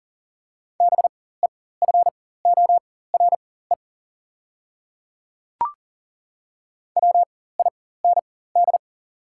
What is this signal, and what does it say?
Morse code, heard as a keyed tone, about 40 wpm, spaced as for 19 wpm, 690 Hz. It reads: BEFORE WIND
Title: 4x Character Spacing - Top 500 Words - 40wpm